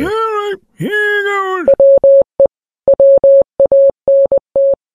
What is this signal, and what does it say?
Morse code, heard as a keyed tone, about 20 wpm, 550 Hz, with speech. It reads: WE WANT